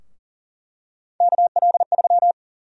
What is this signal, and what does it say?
Morse code, heard as a keyed tone, about 40 wpm, 700 Hz. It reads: KL3
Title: USA Call Sign Prefixes - 40wpm